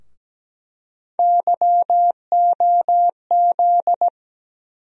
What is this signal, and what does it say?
Morse code, heard as a keyed tone, about 17 wpm, 700 Hz. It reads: YOZ
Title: USA Call Sign Postfixes - 3 Letters 17wpm